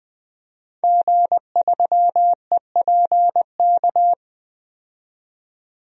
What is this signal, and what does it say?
Morse code, heard as a keyed tone, about 20 wpm, 700 Hz. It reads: G3EPK
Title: DXCC-100 Call Signs - 5 Characters - 20wpm